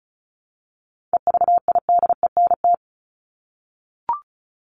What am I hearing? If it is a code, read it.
Morse code, heard as a keyed tone, about 34 wpm, 700 Hz. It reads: EVIDENT
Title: LY Suffix - Top 100 Words with Base Words 35wpm